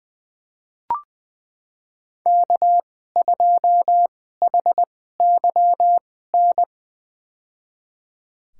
Morse code, heard as a keyed tone, about 20 wpm, spaced as for 15 wpm, 700 Hz. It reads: K2HYN
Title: USA Call Signs - 2x Character Spacing - 20wpm